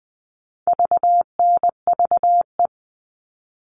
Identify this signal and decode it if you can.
Morse code, heard as a keyed tone, about 20 wpm, 700 Hz. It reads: VNVE
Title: All Cut Numbers - 4-Digit 20wpm